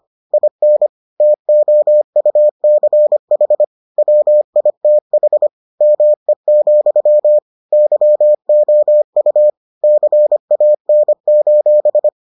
Morse code, heard as a keyed tone, about 25 wpm, 600 Hz. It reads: IN TOUCH WITH ME, YOU CAN: